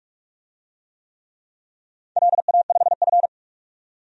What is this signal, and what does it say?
Morse code, heard as a keyed tone, about 46 wpm, 700 Hz. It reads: LA5F
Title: DXCC-100 Call Signs - 4 Characters - 45wpm